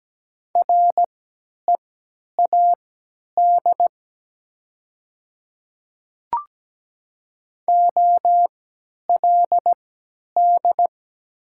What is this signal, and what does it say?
Morse code, heard as a keyed tone, about 17 wpm, spaced as for 10 wpm, 700 Hz. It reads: READ OLD
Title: 3x Character Spacing - Top 200 Words - 17wpm